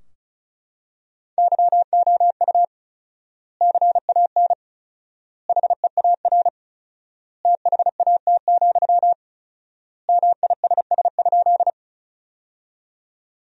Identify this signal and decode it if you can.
Morse code, heard as a keyed tone, about 36 wpm, 700 Hz. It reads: YOU CAN HEAR THAT, MISS?
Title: 4x Word-Spacing - Sentences from Top 500 Words - 35wpm